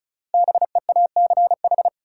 Morse code, heard as a keyed tone, about 36 wpm, 700 Hz. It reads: BEACH
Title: Sentences from Top 1500 Words - New Word Drill Alt - 35wpm